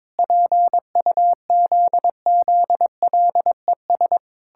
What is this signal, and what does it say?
Morse code, heard as a keyed tone, about 22 wpm, 700 Hz. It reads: PUZZLES
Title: Pangram Challenge v2 - 22wpm